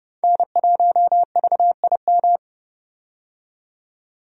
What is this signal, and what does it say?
Morse code, heard as a keyed tone, about 31 wpm, 700 Hz. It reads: N1VIM
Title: USA Call Signs - 5 Characters - 30wpm